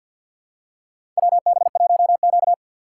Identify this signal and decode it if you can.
Morse code, heard as a keyed tone, about 51 wpm, 700 Hz. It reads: WB1Q
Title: USA Call Signs - 3x Character Spacing - 50wpm